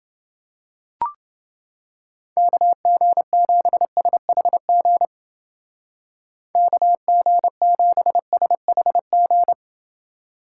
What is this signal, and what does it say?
Morse code, heard as a keyed tone, about 30 wpm, 700 Hz. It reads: KG7SHG KG7SHG